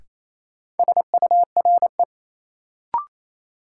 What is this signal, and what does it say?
Morse code, heard as a keyed tone, about 28 wpm, 700 Hz. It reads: SURE
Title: Sentences from Top 300 Words - New Word Drill - 28wpm